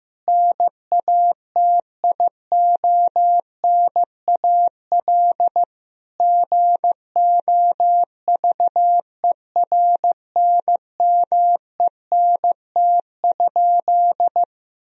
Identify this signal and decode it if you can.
Morse code, heard as a keyed tone, about 16 wpm, 700 Hz. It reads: NATIONAL GOVERNMENT?
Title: Sentences from Top 1000 Words - Review Part 4 - 15wpm